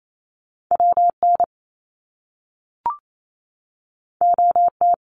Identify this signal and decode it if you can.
Morse code, heard as a keyed tone, about 28 wpm, 700 Hz. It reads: W N O T
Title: Top Two Letter Combinations in Words 28wpm